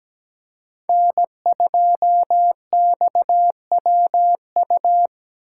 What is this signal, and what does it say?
Morse code, heard as a keyed tone, about 20 wpm, 700 Hz. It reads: N2XWU
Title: Speed Racing - USA 1x3 Call Signs 17wpm